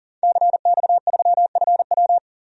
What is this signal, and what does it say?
Morse code, heard as a keyed tone, about 40 wpm, 700 Hz. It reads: CX3FW
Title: DXCC-100 Call Signs - 5 Characters - 40wpm